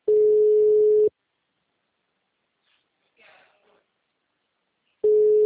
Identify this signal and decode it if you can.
Ringback tone (Europe, 425 Hz), with speech.
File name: out- -2014- - - .15496